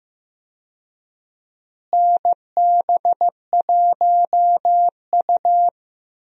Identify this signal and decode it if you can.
Morse code, heard as a keyed tone, about 15 wpm, 700 Hz. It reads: NB1U
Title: USA Call Signs - 4 Characters - Repeated 2x - 15wpm